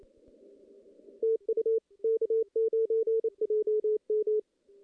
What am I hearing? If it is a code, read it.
Morse code, heard as a keyed tone, about 28 wpm, 410 Hz. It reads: TU K9JM
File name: contests / CW-SS-k5tr- -1600